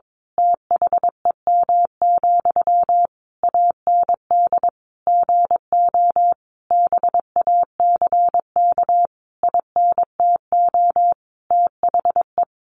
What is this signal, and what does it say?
Morse code, heard as a keyed tone, about 22 wpm, 700 Hz. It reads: THEM, AND GO BACK INTO THE
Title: A Sound Of Thunder 22wpm